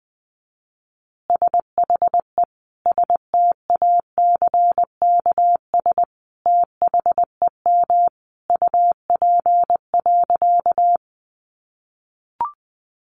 Morse code, heard as a keyed tone, about 20 wpm, 700 Hz. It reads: SHE STACKS THEM UP.
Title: Easy Phrases 20wpm